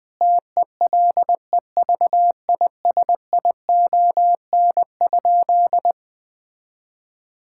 Morse code, heard as a keyed tone, about 21 wpm, 700 Hz. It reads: TELEVISION?
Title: Sentences from Top 1000 Words - Review Part 5 - 20wpm